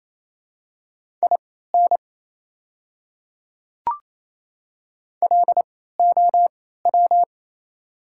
Morse code, heard as a keyed tone, about 28 wpm, spaced as for 16 wpm, 700 Hz. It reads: IN LOW
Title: 3x Character Spacing - Top 200 Words - 28wpm